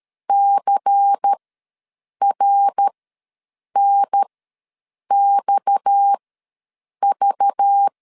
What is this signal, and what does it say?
Morse code, heard as a keyed tone, about 13 wpm, spaced as for 7 wpm, 800 Hz. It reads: CRNXV
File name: LEZIONI / 6 PKX0 / 6 PKX ws 12wpm 3106